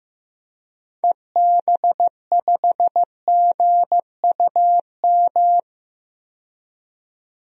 Morse code, heard as a keyed tone, about 15 wpm, 700 Hz. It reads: EB5GUM
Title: DXCC-100 Call Signs - 6 Characters - 15wpm